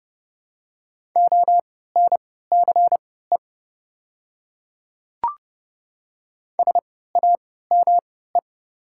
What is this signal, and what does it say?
Morse code, heard as a keyed tone, about 31 wpm, spaced as for 17 wpm, 690 Hz. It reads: ONCE SAME
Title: 3x Character Spacing - Top 500 Words - 30wpm